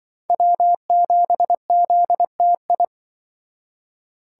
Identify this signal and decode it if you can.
Morse code, heard as a keyed tone, about 25 wpm, 700 Hz. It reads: W7ZTI
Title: Speed Racing - USA 1x3 Call Signs 20wpm